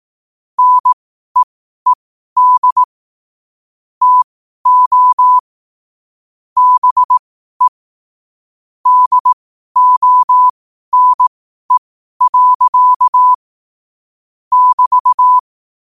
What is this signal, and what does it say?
Morse code, heard as a keyed tone, about 18 wpm, spaced as for 12 wpm, 1000 Hz. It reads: NEED TO BE DONE. <BT>
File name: qotdfiles / 12wpm